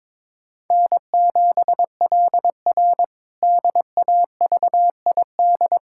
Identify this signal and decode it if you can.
Morse code, heard as a keyed tone, about 22 wpm, 700 Hz. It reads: N7LR DAVID
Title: K1USN SST QSO Practice - 22wpm